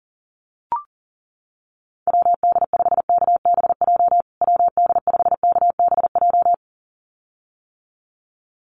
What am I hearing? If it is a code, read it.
Morse code, heard as a keyed tone, about 40 wpm, 700 Hz. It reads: WD5KBJ WD5KBJ
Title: USA Call Signs Repeated 2x 40wpm